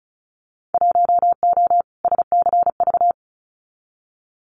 Morse code, heard as a keyed tone, about 34 wpm, 690 Hz. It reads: 1O SCV
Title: Winter Field Day Exchanges 35wpm